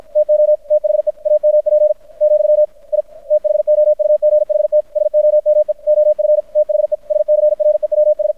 Morse code, heard as a keyed tone, about 53 wpm, 600 Hz. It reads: TO THE AMP , I THOUGHT SOME OF THE SCREWS
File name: archived / CRSnet